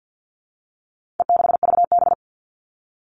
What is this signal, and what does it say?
Morse code, heard as a keyed tone, about 50 wpm, 700 Hz. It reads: E6VB
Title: All Cut Numbers - 4-Digit 50wpm